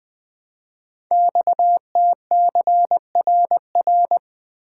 Morse code, heard as a keyed tone, about 20 wpm, 700 Hz. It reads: XTCRR
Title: Code Groups - 5 Letters - 20wpm